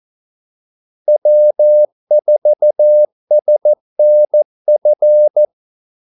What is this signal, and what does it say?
Morse code, heard as a keyed tone, about 14 wpm, 600 Hz. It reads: W4SNF
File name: Callsigns-14wpm